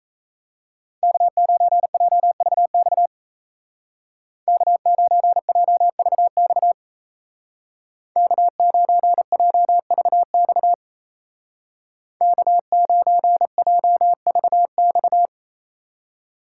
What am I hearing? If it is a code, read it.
Morse code, heard as a keyed tone, about 35 wpm, 700 Hz. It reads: K9JVX K9JVX K9JVX K9JVX
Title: Speed Racing - USA 1x3 Call Signs 28wpm